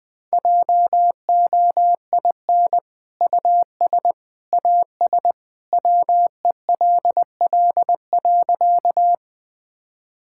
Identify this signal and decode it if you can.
Morse code, heard as a keyed tone, about 20 wpm, 700 Hz. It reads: JOIN US AS WELL.
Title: Sentences with 2-Word Phrases - 20wpm